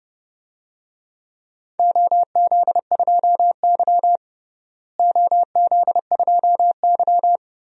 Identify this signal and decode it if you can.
Morse code, heard as a keyed tone, about 30 wpm, 700 Hz. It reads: OZ2Y OZ2Y
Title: DXCC-100 Call Signs - 4 Characters - Repeated 3x - 30wpm